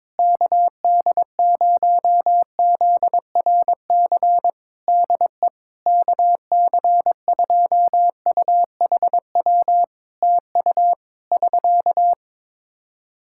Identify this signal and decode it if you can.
Morse code, heard as a keyed tone, about 22 wpm, 700 Hz. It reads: KD0ZRC DE KC2UHW TU <SK>
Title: Practice QSO - Exchange 4 - Ending - 22wpm